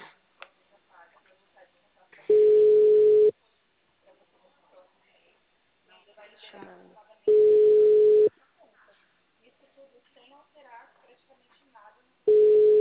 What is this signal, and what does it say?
Ringback tone (Europe, 425 Hz), with speech.
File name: out- -2001- - - .267